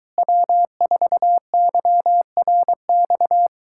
Morse code, heard as a keyed tone, about 23 wpm, 700 Hz. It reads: W4YRX